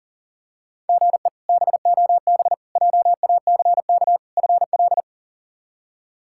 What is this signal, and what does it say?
Morse code, heard as a keyed tone, about 40 wpm, 700 Hz. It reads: GE BOB JACK FL
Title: K1USN SST QSO Practice - 40wpm